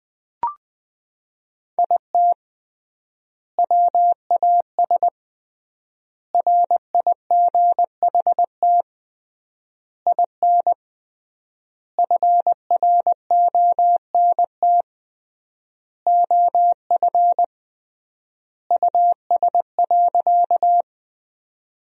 Morse code, heard as a keyed tone, about 21 wpm, 700 Hz. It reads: IT WAS RIGHT IN FRONT OF US.
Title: 3x Word-Spacing - Sentences from Top 500 Words - 20wpm